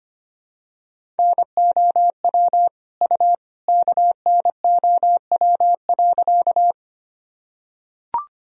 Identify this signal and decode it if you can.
Morse code, heard as a keyed tone, about 25 wpm, 700 Hz. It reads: NOW U KNOW.
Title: Sentences from Top 100 Words and Abbreviations 25wpm